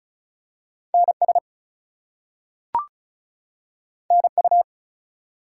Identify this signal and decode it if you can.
Morse code, heard as a keyed tone, about 35 wpm, 700 Hz. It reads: NS NU
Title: US States and Canadian Provinces 35wpm